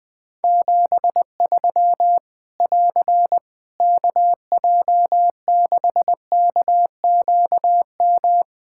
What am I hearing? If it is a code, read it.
Morse code, heard as a keyed tone, about 20 wpm, 700 Hz. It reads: 73 <AR> KJ6KQM
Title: Practice QSO - Exchange 4 - Ending - 20wpm